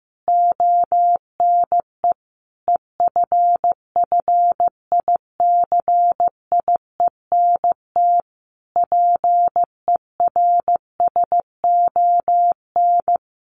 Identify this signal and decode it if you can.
Morse code, heard as a keyed tone, about 15 wpm, 700 Hz. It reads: ONE EFFICIENT PERSON
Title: Sentences from Top 1300 Words 15wpm